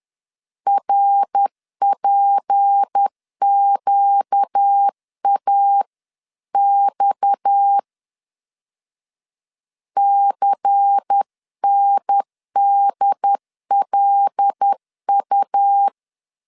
Morse code, heard as a keyed tone, about 10 wpm, 800 Hz. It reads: RPQAX CNDLU